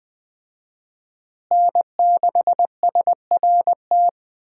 Morse code, heard as a keyed tone, about 20 wpm, 700 Hz. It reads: N6SRT
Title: USA Call Signs - 5 Characters - Repeated 2x - 20wpm